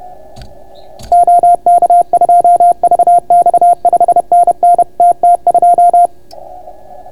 Morse code, heard as a keyed tone, about 31 wpm, 690 Hz. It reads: OK2VX 5NN TT2